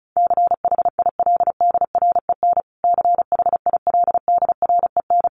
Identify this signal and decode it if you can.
Morse code, heard as a keyed tone, about 34 wpm, 700 Hz. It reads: CHILDREN CHILDREN
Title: Top 500 Words Repeated 35wpm